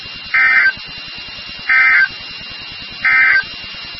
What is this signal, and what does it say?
Emergency Alert System (SAME) tones, heard as harsh buzzing bursts. It is heard from 0.2 to 3.5 s.
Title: NOAA Weather Radio WXK36